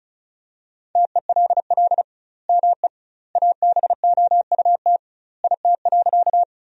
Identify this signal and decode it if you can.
Morse code, heard as a keyed tone, about 36 wpm, 700 Hz. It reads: TELL ME ABOUT IT.
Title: 2x Word-Spacing - Sentences from Top 200 Words - 35wpm